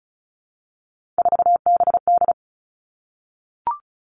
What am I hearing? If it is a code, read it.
Morse code, heard as a keyed tone, about 36 wpm, 700 Hz. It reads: 4BD
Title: Three Challenging Letter-Number Combinations 35wpm